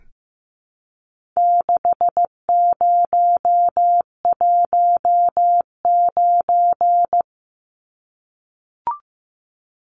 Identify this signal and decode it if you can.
Morse code, heard as a keyed tone, about 15 wpm, 700 Hz. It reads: 6019
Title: Numbers - 4-Digit 15wpm